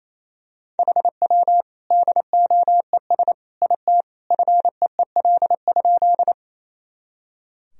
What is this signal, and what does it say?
Morse code, heard as a keyed tone, about 27 wpm, 700 Hz. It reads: HW DOES IT FEEL?
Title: Sentences from Top 400 Words and Abbreviations 28wpm